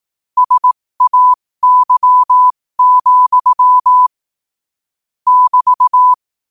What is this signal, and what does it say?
Morse code, heard as a keyed tone, about 18 wpm, spaced as for 15 wpm, 1000 Hz. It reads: SAY, <BT>
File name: qotdfiles / 15wpm